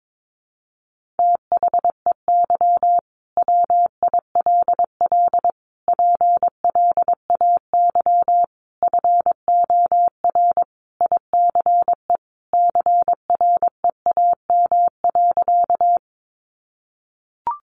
Morse code, heard as a keyed tone, about 22 wpm, 700 Hz. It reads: THEY WILL PLAY FOR ICE CREAM.
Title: Sentences from Top 1500 Words 22wpm